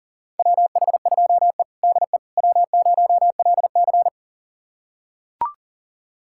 Morse code, heard as a keyed tone, about 40 wpm, 700 Hz. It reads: WH2E DE W0LC
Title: USA Callsign DE Callsign - 4 Characters - 40wpm